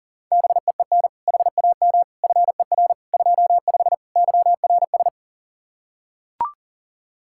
Morse code, heard as a keyed tone, about 40 wpm, 700 Hz. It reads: BEEN HAM FER 25 YRS
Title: Practice QSO - Exchange 3 - More Info - 40wpm